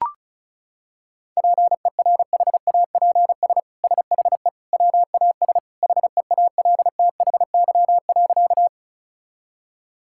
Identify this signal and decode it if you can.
Morse code, heard as a keyed tone, about 34 wpm, 700 Hz. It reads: PERHAPS SHE WAS HEALTHY.
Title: Sentences from Top 1000 Words - Review Part 5 - 35wpm